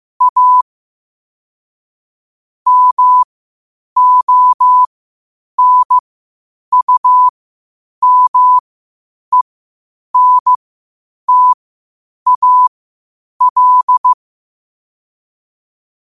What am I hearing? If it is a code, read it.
Morse code, heard as a keyed tone, about 15 wpm, spaced as for 8 wpm, 1000 Hz. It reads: A MONUMENTAL